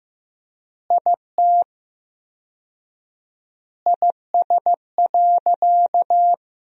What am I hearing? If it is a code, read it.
Morse code, heard as a keyed tone, about 15 wpm, 700 Hz. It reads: IT IS.
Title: 4x Word-Spacing - Sentences from Top 100 Words - 15wpm